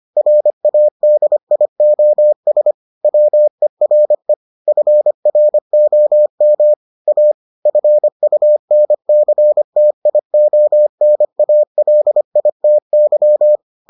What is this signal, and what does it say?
Morse code, heard as a keyed tone, about 25 wpm, 600 Hz. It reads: RADIOS WERE FROM A FUNCTIONALITY